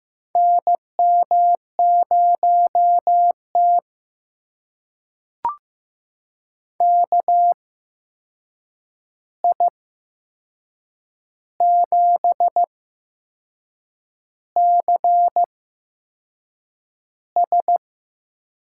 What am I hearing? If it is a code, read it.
Morse code, heard as a keyed tone, about 15 wpm, spaced as for 6 wpm, 700 Hz. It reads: NM0T KI7CS
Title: USA Call Signs - 8x Character Spacing - 15wpm